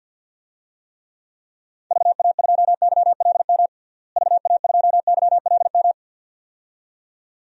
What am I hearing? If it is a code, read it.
Morse code, heard as a keyed tone, about 46 wpm, 700 Hz. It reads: VA2YLM VA2YLM
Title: DXCC-100 Call Signs - 6 Characters - Repeated 2x - 50wpm